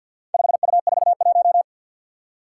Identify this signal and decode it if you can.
Morse code, heard as a keyed tone, about 48 wpm, 700 Hz. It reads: 5U41
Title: Four Challenging Letter-Number Combinations 50wpm